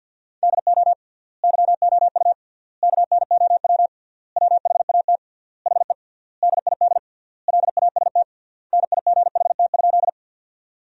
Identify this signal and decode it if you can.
Morse code, heard as a keyed tone, about 50 wpm, 700 Hz. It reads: DO YOU KNOW WHAT HE DID LAST NIGHT?
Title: 3x Word-Spacing - Sentences from Top 300 Words - 50wpm